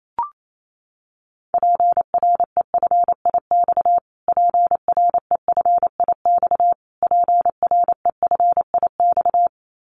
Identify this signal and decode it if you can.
Morse code, heard as a keyed tone, about 28 wpm, 700 Hz. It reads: PREFIX PREFIX PREFIX
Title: Callsign Prefixes for Top 25 DXCC - Repeated 3x - 28wpm